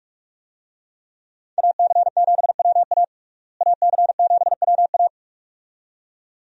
Morse code, heard as a keyed tone, about 47 wpm, 700 Hz. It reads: AC7WA AC7WA
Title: USA Call Signs - 5 Characters - Repeated 2x - 45wpm